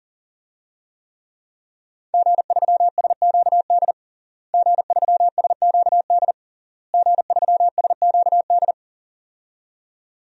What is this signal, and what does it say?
Morse code, heard as a keyed tone, about 37 wpm, 700 Hz. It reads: G3SQD G3SQD G3SQD
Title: DXCC-100 Call Signs - 5 Characters - Repeated 3x - 40wpm